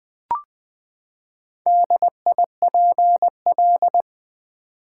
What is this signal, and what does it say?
Morse code, heard as a keyed tone, about 20 wpm, 700 Hz. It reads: DIPL
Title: Build A Word - 4 Letters At A Time - 10-Letter Words - 20wpm